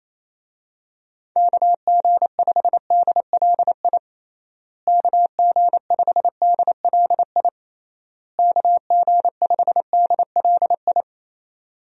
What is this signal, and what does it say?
Morse code, heard as a keyed tone, about 28 wpm, 700 Hz. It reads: KG5DLI KG5DLI KG5DLI
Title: USA Call Signs - 6 Characters - Repeated 3x - 28wpm